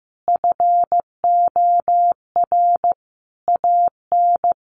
Morse code, heard as a keyed tone, about 15 wpm, 700 Hz. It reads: FOR AN
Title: Sentences from Top 1500 Words 15wpm